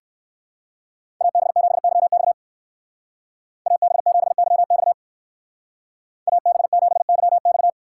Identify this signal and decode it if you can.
Morse code, heard as a keyed tone, about 64 wpm, 700 Hz. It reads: AB7YX AB7YX AB7YX